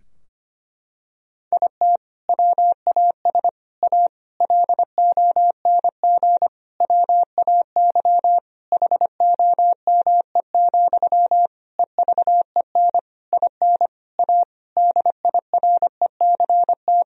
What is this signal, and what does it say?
Morse code, heard as a keyed tone, about 25 wpm, 700 Hz. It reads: IT WAS A LONG WAY HOME, EVEN IN A DIRECT
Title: Sentences from Top 1000 Words - Review Part 2 - 25wpm